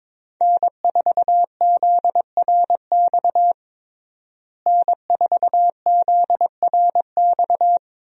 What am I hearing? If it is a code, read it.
Morse code, heard as a keyed tone, about 22 wpm, 700 Hz. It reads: N4ZRX N4ZRX
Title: USA Call Signs - 5 Characters - Repeated 3x - 22wpm